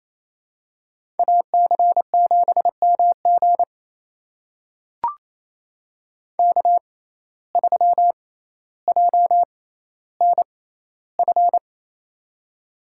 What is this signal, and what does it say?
Morse code, heard as a keyed tone, about 28 wpm, spaced as for 14 wpm, 700 Hz. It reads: AC7MG K3JNF
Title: USA Call Signs - 5 Characters - 6x Character Spacing - 28wpm